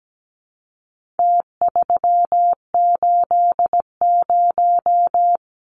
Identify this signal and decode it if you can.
Morse code, heard as a keyed tone, about 17 wpm, 700 Hz. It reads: T380
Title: Numbers - Contest Style - 4-Digit 17wpm